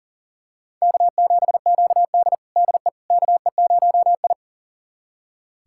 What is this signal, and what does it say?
Morse code, heard as a keyed tone, about 40 wpm, 700 Hz. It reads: K7QD DE KE0I
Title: USA Callsign DE Callsign - 4 Characters - 40wpm